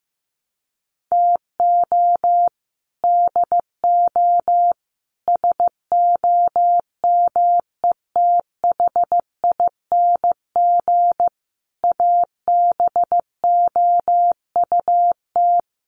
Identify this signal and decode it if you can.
Morse code, heard as a keyed tone, about 15 wpm, 700 Hz. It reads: TO DO SOMETHING ABOUT
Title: Sets of 5 Words 15wpm